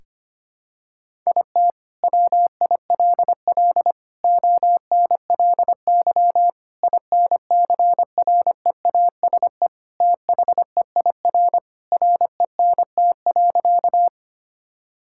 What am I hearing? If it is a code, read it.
Morse code, heard as a keyed tone, about 25 wpm, 700 Hz. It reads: IT WILL ONLY INCREASE THEIR RENT.
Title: Sentences from Top 1000 Words - Review Part 5 - 25wpm